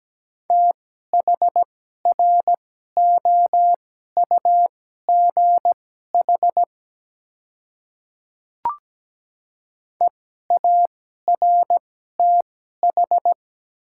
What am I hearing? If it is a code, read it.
Morse code, heard as a keyed tone, about 17 wpm, spaced as for 12 wpm, 700 Hz. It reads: THROUGH EARTH
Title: 2x Character Spacing - Top 300 Words - 17wpm